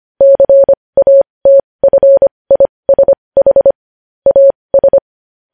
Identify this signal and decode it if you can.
Morse code, heard as a keyed tone, about 25 wpm, spaced as for 20 wpm, 550 Hz. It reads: CATFISH AS
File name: podcast / News headlines-1x-25x20